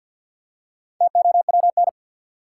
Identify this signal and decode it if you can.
Morse code, heard as a keyed tone, about 52 wpm, 700 Hz. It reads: TOWN
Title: Four Letter Words - 50wpm